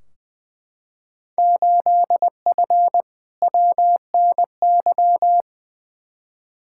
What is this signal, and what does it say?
Morse code, heard as a keyed tone, about 20 wpm, 700 Hz. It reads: 8F WNY
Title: Field Day Exchanges Encore - 20wpm